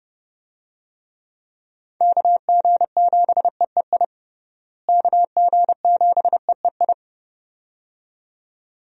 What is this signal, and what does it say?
Morse code, heard as a keyed tone, about 30 wpm, 700 Hz. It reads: KG7EEI KG7EEI
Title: USA Call Signs - 6 Characters - Repeated 2x - 30wpm